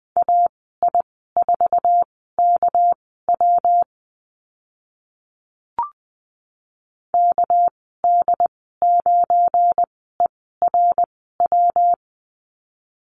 Morse code, heard as a keyed tone, about 20 wpm, spaced as for 15 wpm, 700 Hz. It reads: AI4KW KD9ERW
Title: 200 Lesson 32g - Letter k - Call Signs 20wpm